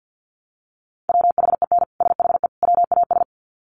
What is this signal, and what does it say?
Morse code, heard as a keyed tone, about 50 wpm, 700 Hz. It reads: WHEN SHE WAS
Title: Sets of 3 Words 50wpm